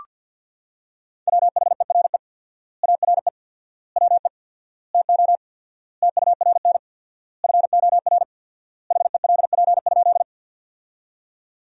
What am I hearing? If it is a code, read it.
Morse code, heard as a keyed tone, about 47 wpm, 700 Hz. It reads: WHERE ARE WE TO TURN FOR HELP?
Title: 4x Word-Spacing - Sentences from Top 200 Words - 50wpm